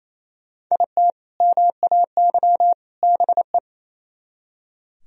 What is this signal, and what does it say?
Morse code, heard as a keyed tone, about 33 wpm, 700 Hz. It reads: IT MAY BE
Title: Speed-Racing - Sets of 3 Words 28wpm